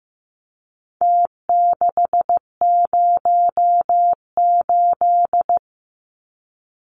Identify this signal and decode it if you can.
Morse code, heard as a keyed tone, about 15 wpm, 700 Hz. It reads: T608
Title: Numbers - Contest Style - 4-Digit 15wpm